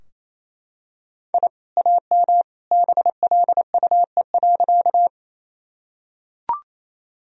Morse code, heard as a keyed tone, about 28 wpm, 700 Hz. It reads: I AM BLUE.
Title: Sentences from Top 1000 Words - Review Part 3 - 28wpm